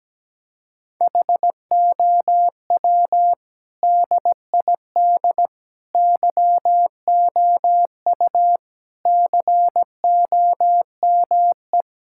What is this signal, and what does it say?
Morse code, heard as a keyed tone, about 17 wpm, 700 Hz. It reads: HOW DID YOU COME
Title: Sentences from Top 1600 Words 17wpm